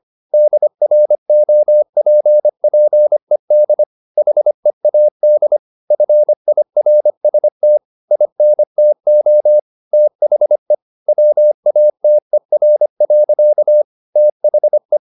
Morse code, heard as a keyed tone, about 25 wpm, 600 Hz. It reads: DROPPED HEAD FIRST INTO THE WATER. THE
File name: the old man and the sea part 1 25wpm